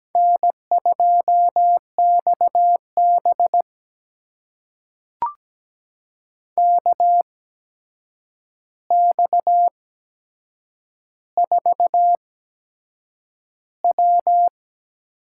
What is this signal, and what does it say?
Morse code, heard as a keyed tone, about 17 wpm, spaced as for 7 wpm, 700 Hz. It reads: N2XB KX4W
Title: USA Call Signs - 4 Characters - 8x Character Spacing - 17wpm